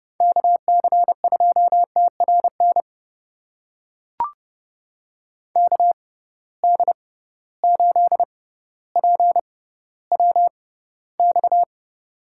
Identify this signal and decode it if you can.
Morse code, heard as a keyed tone, about 30 wpm, spaced as for 15 wpm, 700 Hz. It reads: KC2TRN KD8PWX
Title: USA Call Signs - 6 Characters - 6x Character Spacing - 30wpm